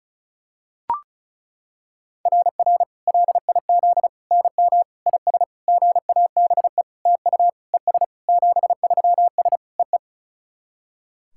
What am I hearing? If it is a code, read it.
Morse code, heard as a keyed tone, about 36 wpm, 690 Hz. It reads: RR LIZ NM IS GABE TU ES 73S EE
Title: SOTA QSO Practice 35wpm